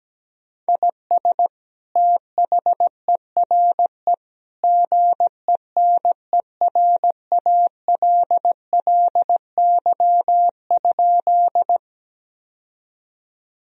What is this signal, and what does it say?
Morse code, heard as a keyed tone, about 17 wpm, 700 Hz. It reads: IS THERE GENERALLY?
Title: Sentences from Top 1000 Words - Review Part 4 - 17wpm